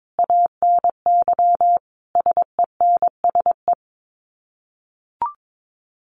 Morse code, heard as a keyed tone, about 22 wpm, 700 Hz. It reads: ANY SENSE
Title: Sets of 5 Words Encore2 22wpm